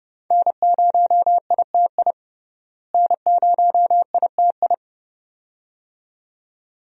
Morse code, heard as a keyed tone, about 32 wpm, 700 Hz. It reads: N0ITI N0ITI
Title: USA Call Signs - 5 Characters - Repeated 2x - 30wpm